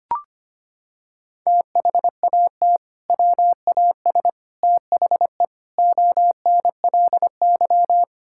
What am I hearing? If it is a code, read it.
Morse code, heard as a keyed tone, about 25 wpm, 700 Hz. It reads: THAT WAS THE ONLY
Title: Sentences from Top 1600 Words 25wpm